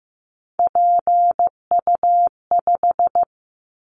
Morse code, heard as a keyed tone, about 15 wpm, 700 Hz. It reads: PU5
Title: Callsign Prefixes for Top 100 DXCC - 15wpm